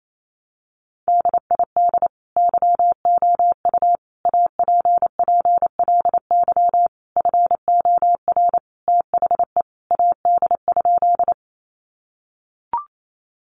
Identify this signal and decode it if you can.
Morse code, heard as a keyed tone, about 28 wpm, 700 Hz. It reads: DID YOU APPLY FOR THE AD?
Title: Sentences from Top 1300 Words 28wpm